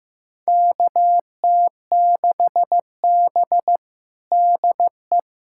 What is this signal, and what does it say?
Morse code, heard as a keyed tone, about 15 wpm, 700 Hz. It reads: KT6B DE
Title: USA Callsign DE Callsign - 4 Characters - 15wpm